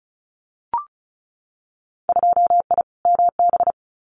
Morse code, heard as a keyed tone, about 36 wpm, 700 Hz. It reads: 2I MB
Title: Winter Field Day Exchanges Encore 35wpm